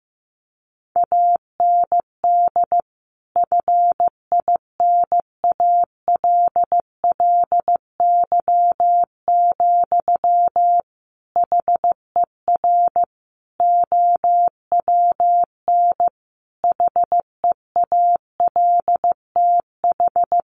Morse code, heard as a keyed tone, about 15 wpm, 700 Hz. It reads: AND FINALLY, HER OWN HEALTH
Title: Sentences from Top 1100 Words 15wpm